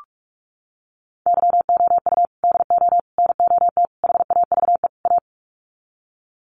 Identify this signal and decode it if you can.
Morse code, heard as a keyed tone, about 44 wpm, 700 Hz. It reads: YOU DO NOT HAVE A
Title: Sets of 5 Words 45wpm